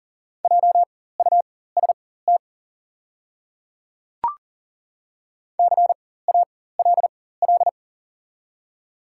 Morse code, heard as a keyed tone, about 39 wpm, spaced as for 18 wpm, 700 Hz. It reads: JUST CALL
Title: 4x Character Spacing - Top 300 Words - 40wpm